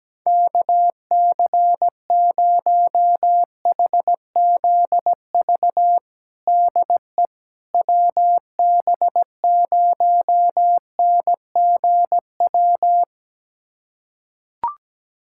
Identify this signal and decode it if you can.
Morse code, heard as a keyed tone, about 17 wpm, 700 Hz. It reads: KC0HZV DE WB0NGW
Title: USA Callsign DE Callsign - 6 Characters - 17wpm